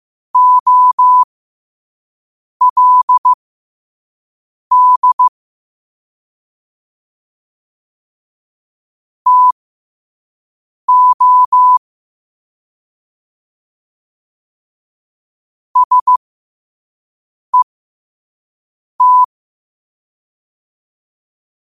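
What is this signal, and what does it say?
Morse code, heard as a keyed tone, about 15 wpm, spaced as for 5 wpm, 1000 Hz. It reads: OLD TO SET
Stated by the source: Morse code practice at 5 wpm.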